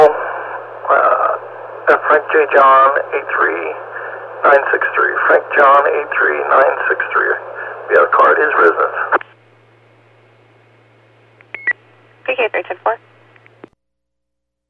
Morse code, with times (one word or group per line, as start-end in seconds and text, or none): none